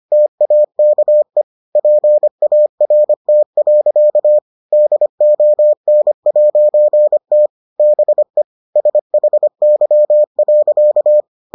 Morse code, heard as a keyed tone, about 25 wpm, 600 Hz. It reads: TAKE PART. DON'T BE SHY.